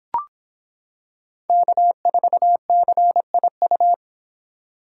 Morse code, heard as a keyed tone, about 25 wpm, 700 Hz. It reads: K4CIU